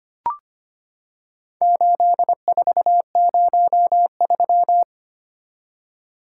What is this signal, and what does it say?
Morse code, heard as a keyed tone, about 25 wpm, 700 Hz. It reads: 8403
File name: practice / 25wpm / Four Letter-Number Combinations - 25wpm